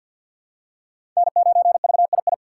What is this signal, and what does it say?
Morse code, heard as a keyed tone, about 52 wpm, 700 Hz. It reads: N9VII
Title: USA Call Signs - 5 Characters - 2x Character Spacing - 50wpm